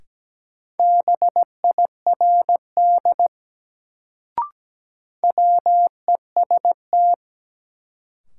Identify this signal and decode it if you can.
Morse code, heard as a keyed tone, about 17 wpm, 700 Hz. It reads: BIRD WEST
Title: Sentences from Top 400 Words - New Word Drill - 17wpm